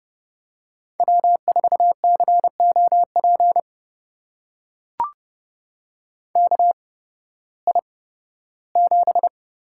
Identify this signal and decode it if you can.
Morse code, heard as a keyed tone, about 32 wpm, spaced as for 16 wpm, 700 Hz. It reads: W4COP KI7
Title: USA Call Signs - 8x Character Spacing - 30wpm